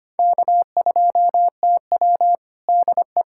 Morse code, heard as a keyed tone, about 25 wpm, 700 Hz. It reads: K2TW DE
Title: Practice QSO - Exchange 4 - Ending - 25wpm